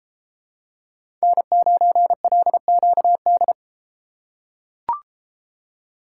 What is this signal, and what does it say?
Morse code, heard as a keyed tone, about 33 wpm, 700 Hz. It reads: N9LQD